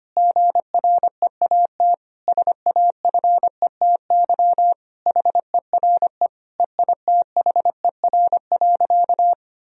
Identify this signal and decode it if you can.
Morse code, heard as a keyed tone, about 25 wpm, 700 Hz. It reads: GREAT SAFETY HERE EITHER.
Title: Sentences from Top 1000 Words - Review Part 4 - 25wpm